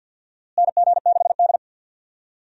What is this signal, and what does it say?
Morse code, heard as a keyed tone, about 51 wpm, 700 Hz. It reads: NG6D
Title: USA Call Signs - 4 Characters - Repeated 3x - 50wpm